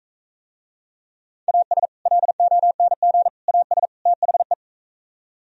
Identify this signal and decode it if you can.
Morse code, heard as a keyed tone, about 45 wpm, 700 Hz. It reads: AS LONG AS THE